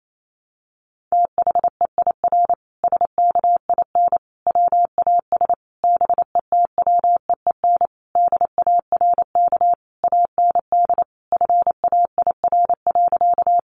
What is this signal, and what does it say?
Morse code, heard as a keyed tone, about 28 wpm, 700 Hz. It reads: THEIR SKIN WAS BETWEEN DARK AND FAIR.
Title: Sentences from Top 1300 Words 28wpm